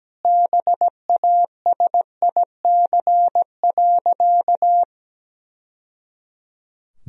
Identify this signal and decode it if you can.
Morse code, heard as a keyed tone, about 17 wpm, 700 Hz. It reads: BASIC.
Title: Sentences from Top 1000 Words - Review Part 3 - 17wpm